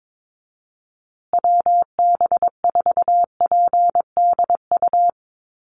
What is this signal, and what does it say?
Morse code, heard as a keyed tone, about 22 wpm, 700 Hz. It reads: WB4PDU
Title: USA Call Signs Repeated 2x 22wpm